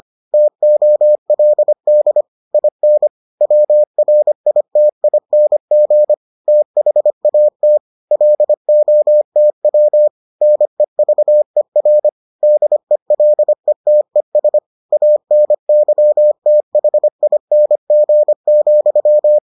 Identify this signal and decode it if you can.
Morse code, heard as a keyed tone, about 25 wpm, 600 Hz. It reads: TOLD IN WRITING THAT LOTW NEVER DELETES ANYTHING,